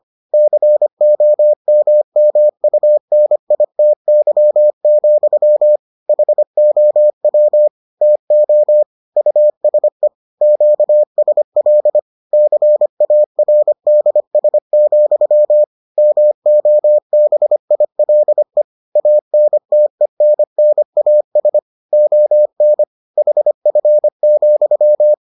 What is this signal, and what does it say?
Morse code, heard as a keyed tone, about 25 wpm, 600 Hz. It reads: COMMUNITY, HOW TO USE QSL CARDS, MOBILE ANTENNAS ON HF,